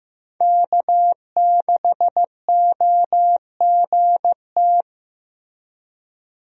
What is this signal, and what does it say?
Morse code, heard as a keyed tone, about 15 wpm, 700 Hz. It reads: K6OGT
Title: USA Call Signs - 5 Characters - Repeated 3x - 15wpm